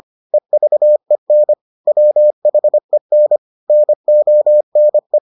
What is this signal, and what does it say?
Morse code, heard as a keyed tone, about 25 wpm, 600 Hz. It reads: EVEN WHEN NONE